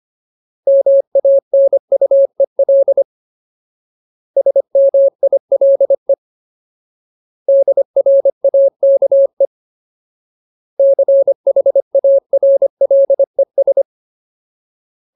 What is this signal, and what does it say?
Morse code, heard as a keyed tone, about 25 wpm, 550 Hz. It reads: MANUEL SMILE DRAKE CHARLES
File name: Ultimate Word List / 25WPM Ultimate